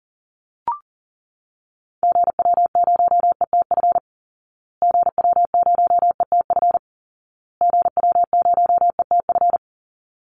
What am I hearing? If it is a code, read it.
Morse code, heard as a keyed tone, about 40 wpm, 700 Hz. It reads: GW0ETF GW0ETF GW0ETF
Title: Call Signs Repeated 2x - 40wpm